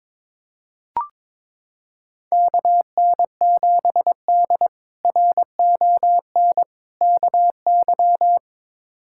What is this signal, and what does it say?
Morse code, heard as a keyed tone, about 22 wpm, 700 Hz. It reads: KN7D RON KY
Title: K1USN SST QSO Practice - 22wpm